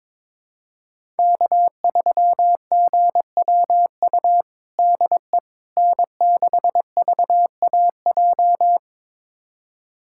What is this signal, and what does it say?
Morse code, heard as a keyed tone, about 22 wpm, 700 Hz. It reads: K3GWU DE N6VAJ
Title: USA Callsign DE Callsign - 5 Characters - 22wpm